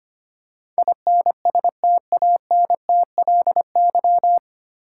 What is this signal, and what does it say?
Morse code, heard as a keyed tone, about 25 wpm, 700 Hz. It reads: INSTANTLY
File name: practice / 25wpm / 9 Letter Words - First 3 Letters Given - 25wpm